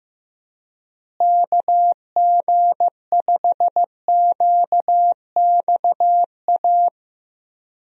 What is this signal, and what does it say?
Morse code, heard as a keyed tone, about 15 wpm, 700 Hz. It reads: KG5QXA
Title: USA Call Signs - 6 Characters - Repeated 3x - 15wpm